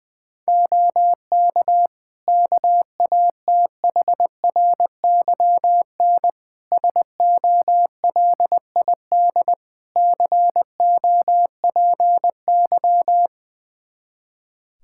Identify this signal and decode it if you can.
Morse code, heard as a keyed tone, about 21 wpm, 700 Hz. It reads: OK KATHRYN SOLID COPY
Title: K1USN SST QSO Practice 20wpm